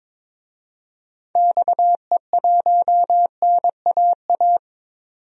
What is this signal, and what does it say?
Morse code, heard as a keyed tone, about 22 wpm, 700 Hz. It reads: XE1NAA
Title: DXCC-100 Call Signs - 6 Characters - 22wpm